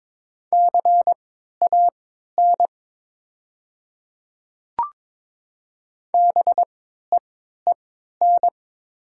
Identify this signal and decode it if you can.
Morse code, heard as a keyed tone, about 22 wpm, spaced as for 13 wpm, 700 Hz. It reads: CAN BEEN
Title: 3x Character Spacing - Top 100 Words - 22wpm